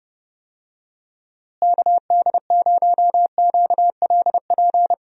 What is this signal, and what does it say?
Morse code, heard as a keyed tone, about 31 wpm, 700 Hz. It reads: KD0QLP
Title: USA Call Signs - 6 Characters - Repeated 3x - 30wpm